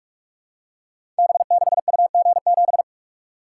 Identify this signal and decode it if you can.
Morse code, heard as a keyed tone, about 44 wpm, 700 Hz. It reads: B6UG7
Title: Code Groups - 5 Letter:Numbers - 45wpm